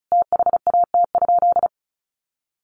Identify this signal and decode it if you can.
Morse code, heard as a keyed tone, about 34 wpm, 700 Hz. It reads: THAT?
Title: Sentences from Top 1500 Words 35wpm